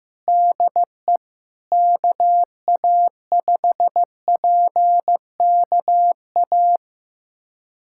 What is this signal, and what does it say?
Morse code, heard as a keyed tone, about 15 wpm, 700 Hz. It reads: DE KA5PKA
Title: USA Callsign DE Callsign - 6 Characters - 15wpm